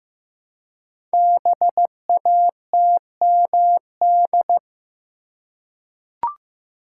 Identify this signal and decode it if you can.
Morse code, heard as a keyed tone, about 15 wpm, 700 Hz. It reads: BATMD
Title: Code Groups - 5 Letters - 15wpm